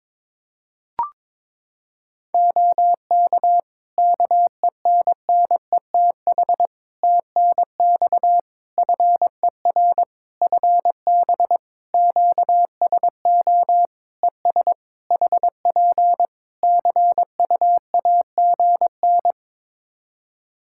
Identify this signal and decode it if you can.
Morse code, heard as a keyed tone, about 22 wpm, 700 Hz. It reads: OK KENNETH TNX FER FB QSO ES HP CUAGN
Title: Practice QSO - Exchange 4 - Ending - 22wpm